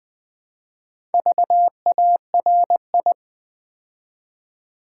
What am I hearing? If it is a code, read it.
Morse code, heard as a keyed tone, about 20 wpm, 700 Hz. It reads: VARI
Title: Build A Word - 2 Letters At A Time - 8-Letter Words - 20wpm